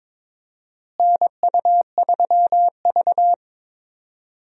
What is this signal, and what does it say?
Morse code, heard as a keyed tone, about 22 wpm, 700 Hz. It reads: NU3V
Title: USA Call Signs - 4 Characters - Repeated 3x - 22wpm